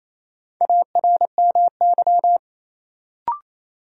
Morse code, heard as a keyed tone, about 28 wpm, 700 Hz. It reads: ARMY
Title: Sentences from Top 800 Words - New Word Drill - 28wpm